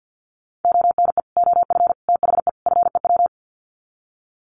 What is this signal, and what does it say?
Morse code, heard as a keyed tone, about 50 wpm, 700 Hz. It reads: ONE OF THE FEW
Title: Sets of 4 Words 50wpm